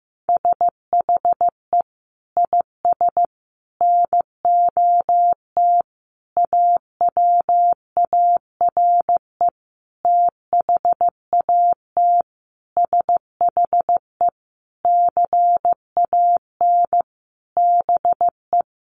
Morse code, heard as a keyed tone, about 15 wpm, 700 Hz. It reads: SHE IS NOT AWARE THAT SHE CAN BE
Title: Sentences from Top 1200 Words 15wpm